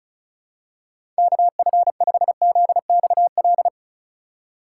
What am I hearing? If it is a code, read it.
Morse code, heard as a keyed tone, about 36 wpm, 700 Hz. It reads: KF5ZXL
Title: USA Call Signs - 6 Characters - 2x Character Spacing - 35wpm